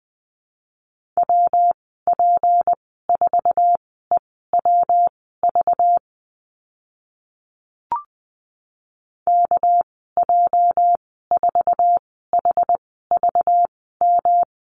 Morse code, heard as a keyed tone, about 20 wpm, spaced as for 14 wpm, 700 Hz. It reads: WP4EWV KJ4HVM
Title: 219 Lesson 35g - Number 0 - Call Signs 20wpm